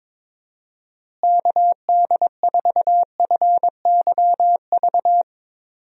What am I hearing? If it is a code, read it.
Morse code, heard as a keyed tone, about 21 wpm, 700 Hz. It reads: KD4FYV
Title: USA Call Signs - 6 Characters - Repeated 3x - 22wpm